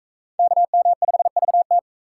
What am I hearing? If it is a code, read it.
Morse code, heard as a keyed tone, about 38 wpm, 700 Hz. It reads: KM5VT